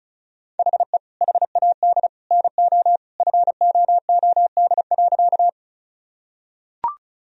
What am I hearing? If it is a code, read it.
Morse code, heard as a keyed tone, about 36 wpm, 700 Hz. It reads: HE HAD NO FOOD.
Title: Sentences from Top 1000 Words - Review Part 1 - 35wpm